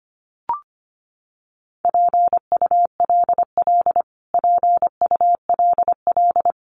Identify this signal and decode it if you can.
Morse code, heard as a keyed tone, about 25 wpm, 700 Hz. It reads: PULL PULL
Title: Top 500 Words Repeated 25wpm